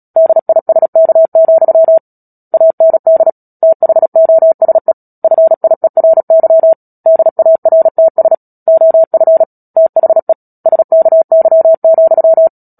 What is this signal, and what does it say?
Morse code, heard as a keyed tone, about 37 wpm, 650 Hz. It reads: DISK, AND THOSE FIERY DARTS OF THE SKY,
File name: Book Two / 10-Chapter 10 - Epilogue - 36 WPM